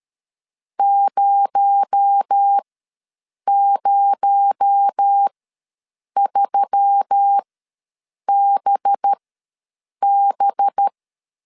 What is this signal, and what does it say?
Morse code, heard as a keyed tone, about 13 wpm, spaced as for 7 wpm, 800 Hz. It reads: 003BB
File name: LEZIONI / 6 PKX0 / 6 PKX0 ws 12wpm 3448